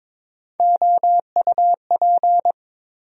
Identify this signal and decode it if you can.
Morse code, heard as a keyed tone, about 21 wpm, 700 Hz. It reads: OUP
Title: USA Call Sign Postfixes - 3 Letters 22wpm